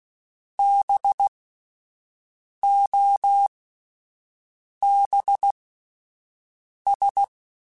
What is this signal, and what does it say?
Morse code, heard as a keyed tone, about 16 wpm, spaced as for 5 wpm, 800 Hz. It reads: BOBS